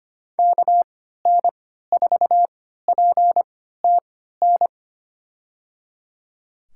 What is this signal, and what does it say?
Morse code, heard as a keyed tone, about 25 wpm, spaced as for 14 wpm, 700 Hz. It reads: KN4PTN
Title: USA Call Signs - 6 Characters - 3x Character Spacing - 25wpm